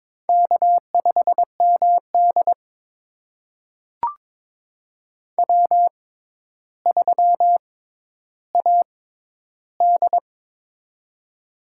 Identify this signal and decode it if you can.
Morse code, heard as a keyed tone, about 22 wpm, spaced as for 11 wpm, 700 Hz. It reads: K5MD W3AD
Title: USA Call Signs - 4 Characters - 6x Character Spacing - 22wpm